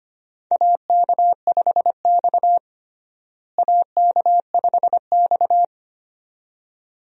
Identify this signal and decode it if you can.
Morse code, heard as a keyed tone, about 25 wpm, 700 Hz. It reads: AK5X AK5X
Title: USA Call Signs - 4 Characters - Repeated 2x - 25wpm